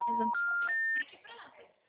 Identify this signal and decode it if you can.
Special information tone, with speech.